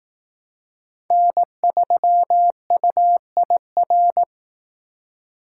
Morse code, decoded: N3UIR

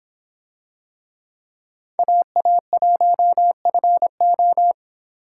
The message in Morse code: AA1FO

